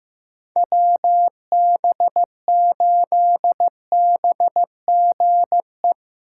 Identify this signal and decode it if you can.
Morse code, heard as a keyed tone, about 17 wpm, 700 Hz. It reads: WB8BGE
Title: Speed Racing - USA 2x3 Call Signs 15wpm